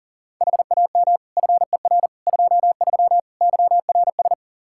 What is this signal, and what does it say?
Morse code, heard as a keyed tone, about 40 wpm, 700 Hz. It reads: HAM FER 23 YRS
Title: K1USN SST QSO Practice 40wpm